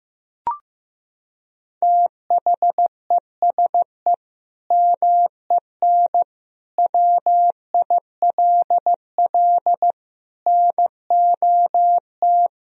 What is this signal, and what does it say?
Morse code, heard as a keyed tone, about 15 wpm, 700 Hz. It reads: THESE MEN WILL NOT
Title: Sentences from Top 1000 Words - Review Part 1 - 15wpm